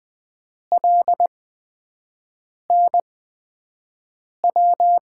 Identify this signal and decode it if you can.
Morse code, heard as a keyed tone, about 20 wpm, spaced as for 5 wpm, 700 Hz. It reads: LNW
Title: USA Call Signs - 6 Characters - 8x Character Spacing - 20wpm